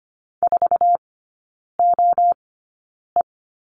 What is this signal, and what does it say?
Morse code, heard as a keyed tone, about 25 wpm, spaced as for 9 wpm, 700 Hz. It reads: 4OE
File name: Lesson 17e - Letter u - Three Characters 25wpm